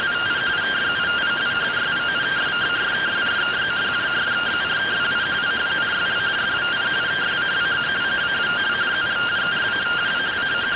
Two-tone teleprinter signal, about 170 Hz shift.